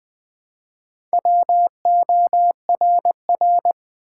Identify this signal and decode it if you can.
Morse code, heard as a keyed tone, about 20 wpm, 700 Hz. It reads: WORR
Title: Build A Word - 1 Letter At A Time - 5-Letter Words - 20wpm